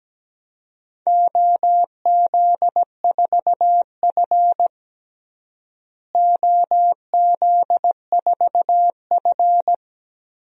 Morse code, heard as a keyed tone, about 17 wpm, 700 Hz. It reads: OZ4F OZ4F
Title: DXCC-100 Call Signs - 4 Characters - Repeated 3x - 17wpm